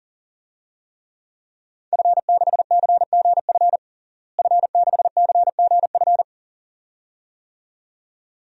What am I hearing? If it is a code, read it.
Morse code, heard as a keyed tone, about 40 wpm, 700 Hz. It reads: F6CGF F6CGF
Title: DXCC-100 Call Signs - 5 Characters - Repeated 2x - 40wpm